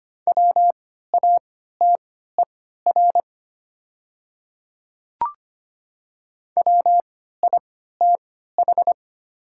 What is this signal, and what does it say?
Morse code, heard as a keyed tone, about 25 wpm, spaced as for 14 wpm, 690 Hz. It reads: WATER WITH
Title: 3x Character Spacing - Top 200 Words - 25wpm